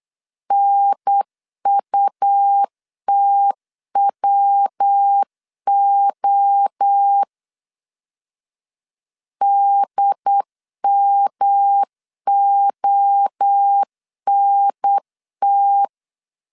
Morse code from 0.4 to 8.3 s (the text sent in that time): NUTWO